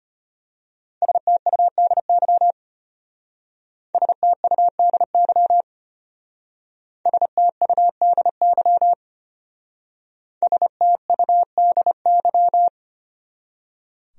Morse code, heard as a keyed tone, about 31 wpm, 700 Hz. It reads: STUDY STUDY STUDY STUDY